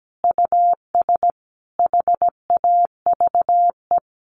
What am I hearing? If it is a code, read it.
Morse code, heard as a keyed tone, about 17 wpm, 700 Hz. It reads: US HAVE
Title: Sentences from Top 700 Words 17wpm